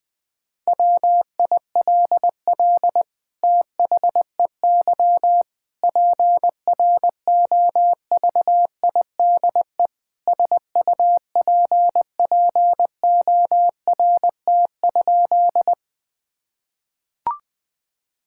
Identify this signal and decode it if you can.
Morse code, heard as a keyed tone, about 20 wpm, 700 Hz. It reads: WILL THEY PROVIDE SUPPORT?
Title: Sentences from Top 1000 Words - Review Part 5 - 20wpm